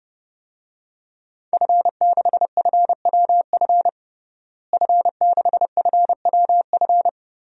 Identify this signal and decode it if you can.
Morse code, heard as a keyed tone, about 31 wpm, 700 Hz. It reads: F6FWF F6FWF
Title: DXCC-100 Call Signs - 5 Characters - Repeated 2x - 30wpm